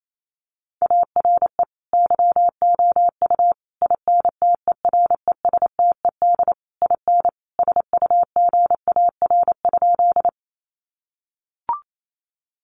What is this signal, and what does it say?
Morse code, heard as a keyed tone, about 28 wpm, 700 Hz. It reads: ARE YOU INTERESTED IN SUGAR?
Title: Sentences from Top 1200 Words 28wpm